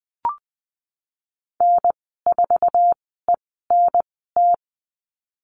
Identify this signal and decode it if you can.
Morse code, heard as a keyed tone, about 20 wpm, spaced as for 14 wpm, 700 Hz. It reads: N4ENT
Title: 048 Lesson 9g - Number 4 - Call Signs 20wpm